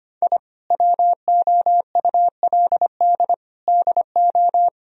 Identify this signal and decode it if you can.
Morse code, heard as a keyed tone, about 25 wpm, 700 Hz. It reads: I WOULD DO
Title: Sentences from Top 1000 Words - Review Part 2 - 25wpm